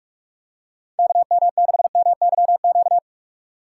Morse code, heard as a keyed tone, about 37 wpm, 700 Hz. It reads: KM6MYQ
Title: Speed Racing - USA 2x3 Call Signs 30wpm